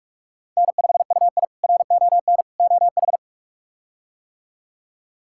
Morse code, heard as a keyed tone, about 42 wpm, 700 Hz. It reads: N5UI RON OH
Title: K1USN SST QSO Practice - 45wpm